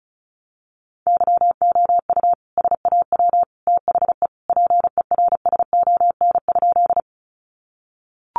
Morse code, heard as a keyed tone, about 34 wpm, 700 Hz. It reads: YOU SAW THE PERSON?
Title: Sentences from Top 400 Words 35wpm